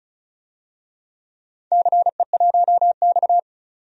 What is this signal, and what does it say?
Morse code, heard as a keyed tone, about 36 wpm, 690 Hz. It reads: CE1X
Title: DXCC-100 Call Signs - 4 Characters - 35wpm